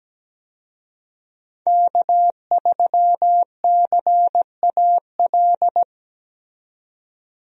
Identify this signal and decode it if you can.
Morse code, heard as a keyed tone, about 17 wpm, 700 Hz. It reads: K3CAL
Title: USA Call Signs - 5 Characters - 17wpm